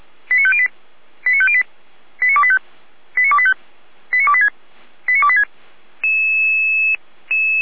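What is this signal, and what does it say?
Five-tone selective call (ZVEI), heard as quick runs of five tones, tones of about 70 ms. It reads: 89508 89508 89285 89285 89286 89286
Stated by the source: ZVEI five-tone call 89262 received by SDR, fire brigade Feuerwehr Orlamünde.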